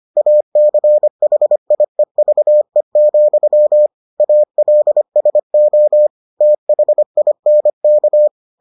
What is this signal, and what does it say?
Morse code, heard as a keyed tone, about 25 wpm, 600 Hz. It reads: ACHIEVE, ALSO THINK